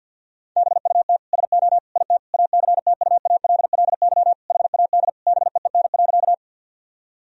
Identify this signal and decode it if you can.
Morse code, heard as a keyed tone, about 50 wpm, 700 Hz. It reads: BUT SO IT ACTUALLY HAD BEEN.
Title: Sentences from Top 1000 Words - Review Part 4 - 50wpm